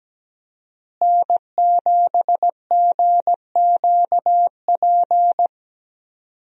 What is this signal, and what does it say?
Morse code, heard as a keyed tone, about 17 wpm, 700 Hz. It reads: N7GQP
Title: USA Call Signs - 5 Characters - Repeated 3x - 17wpm